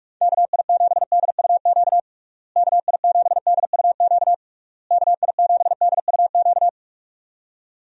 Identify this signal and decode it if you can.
Morse code, heard as a keyed tone, about 46 wpm, 700 Hz. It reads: KI7DUQ KI7DUQ KI7DUQ
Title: USA Call Signs - 6 Characters - Repeated 3x - 45wpm